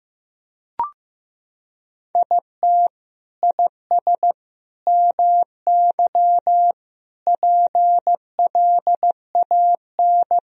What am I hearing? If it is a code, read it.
Morse code, heard as a keyed tone, about 15 wpm, 700 Hz. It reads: IT IS MY PLAN
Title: Sentences from Top 1000 Words - Review Part 2 - 15wpm